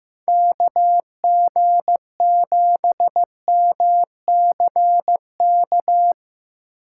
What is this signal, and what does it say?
Morse code, heard as a keyed tone, about 17 wpm, 700 Hz. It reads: KG7MCK
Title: Speed Racing - USA 2x3 Call Signs 15wpm